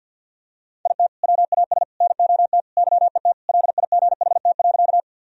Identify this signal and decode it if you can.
Morse code, heard as a keyed tone, about 51 wpm, 700 Hz. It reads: IT WAS NOT YET LIGHT.
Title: Sentences from Top 1000 Words - Review Part 1 - 50wpm